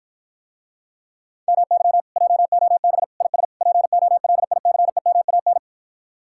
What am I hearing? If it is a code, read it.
Morse code, heard as a keyed tone, about 54 wpm, 700 Hz. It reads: MY JOB IS POLICEMAN